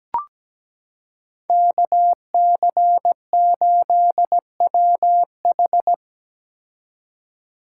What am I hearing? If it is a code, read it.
Morse code, heard as a keyed tone, about 17 wpm, 700 Hz. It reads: KC8WH
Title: K1USN SST QSO Practice - 17wpm